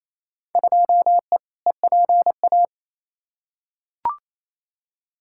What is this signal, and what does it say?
Morse code, heard as a keyed tone, about 28 wpm, 700 Hz. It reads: 2E EPA
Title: Field Day Exchanges Encore - 28wpm